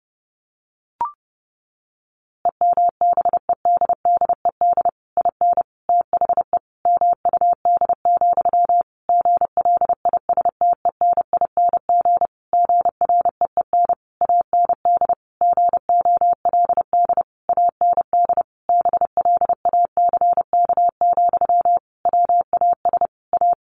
Morse code, heard as a keyed tone, about 28 wpm, 700 Hz. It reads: EMBEDDED IN THE MUD, GLISTENING GREEN AND GOLD AND BLACK, WAS A
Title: A Sound Of Thunder 30wpm